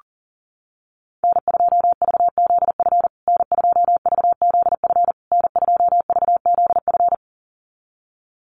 Morse code, heard as a keyed tone, about 40 wpm, 700 Hz. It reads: N2VZF N2VZF N2VZF
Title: USA Call Signs Repeated 3x 40wpm